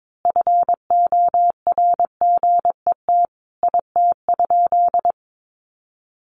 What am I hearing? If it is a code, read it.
Morse code, heard as a keyed tone, about 22 wpm, 700 Hz. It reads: FORGET IT?
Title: Sentences from Top 1200 Words 22wpm